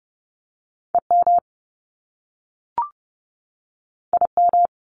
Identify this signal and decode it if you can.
Morse code, heard as a keyed tone, about 30 wpm, 700 Hz. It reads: EM IM
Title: Top Two Letter Combinations in Words 30wpm